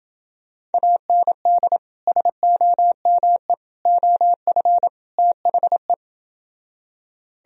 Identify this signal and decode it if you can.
Morse code, heard as a keyed tone, about 30 wpm, 700 Hz. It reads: AND SOME OF THE